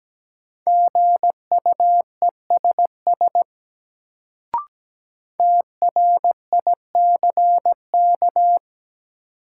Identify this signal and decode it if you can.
Morse code, heard as a keyed tone, about 17 wpm, 700 Hz. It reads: GUESS TRICK
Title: Sentences from Top 1500 Words - New Word Drill - 17wpm